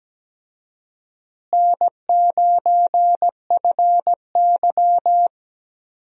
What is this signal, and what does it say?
Morse code, heard as a keyed tone, about 17 wpm, 700 Hz. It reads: N9FY
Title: USA Call Signs - 4 Characters - Repeated 2x - 17wpm